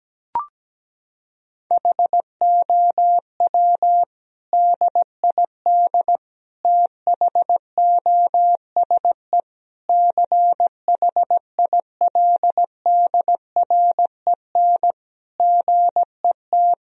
Morse code, heard as a keyed tone, about 17 wpm, 700 Hz. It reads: HOW DID THOSE CHILDREN GET
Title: Sentences from Top 1000 Words - Review Part 1 - 17wpm